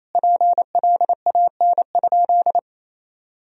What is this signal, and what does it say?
Morse code, heard as a keyed tone, about 28 wpm, 700 Hz. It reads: PLAN?
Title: Sentences from Top 1000 Words - Review Part 2 - 28wpm